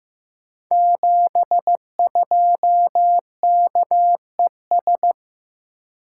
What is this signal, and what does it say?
Morse code, heard as a keyed tone, about 15 wpm, 700 Hz. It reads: 72KES
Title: Code Groups - 5 Letter:Numbers - 15wpm